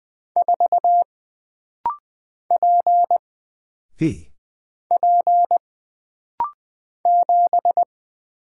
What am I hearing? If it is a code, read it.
Morse code, heard as a keyed tone, about 20 wpm, 700 Hz. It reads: 4 P P 7
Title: Single Character (Letter:Number:Punctuation:Prosign) - Rapid-Fire - 20wpm